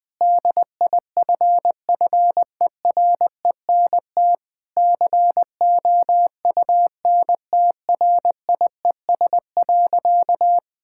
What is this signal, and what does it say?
Morse code, heard as a keyed tone, about 21 wpm, 700 Hz. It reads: DIFFERENT COUNTRIES.
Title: Difficult Sentences with Hints - 20wpm